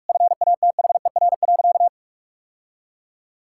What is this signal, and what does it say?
Morse code, heard as a keyed tone, about 46 wpm, 700 Hz. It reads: FATHER.
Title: Sentences from Top 1000 Words - Review Part 6 - 45wpm